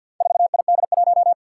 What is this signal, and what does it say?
Morse code, heard as a keyed tone, about 48 wpm, 700 Hz. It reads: 4ID1
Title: Four Challenging Letter-Number Combinations 50wpm